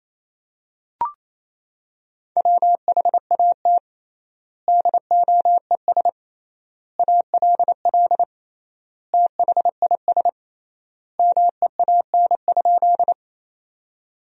Morse code, heard as a keyed tone, about 27 wpm, 700 Hz. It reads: WHAT DOES ALL THIS MEAN?
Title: 3x Word-Spacing - Sentences from Top 200 Words - 28wpm